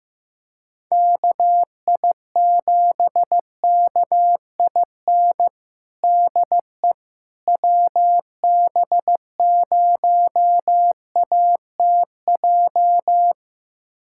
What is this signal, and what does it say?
Morse code, heard as a keyed tone, about 15 wpm, 700 Hz. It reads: KI7KIN DE WB0ATJ
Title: USA Callsign DE Callsign 15wpm